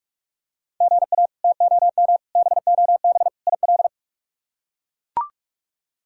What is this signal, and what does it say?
Morse code, heard as a keyed tone, about 44 wpm, 700 Hz. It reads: GA TOM BOB IL
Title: K1USN SST QSO Practice - 45wpm